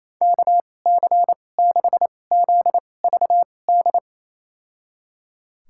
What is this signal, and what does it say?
Morse code, heard as a keyed tone, about 28 wpm, spaced as for 20 wpm, 700 Hz. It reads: KC6ZVD
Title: USA Call Signs - 6 Characters - 2x Character Spacing - 28wpm